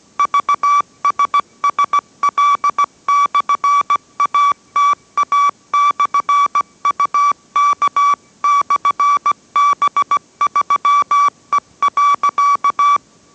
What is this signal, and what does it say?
Morse code, heard as a keyed tone, about 17 wpm, 1150 Hz. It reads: VSSL/ATA/UK/B3E.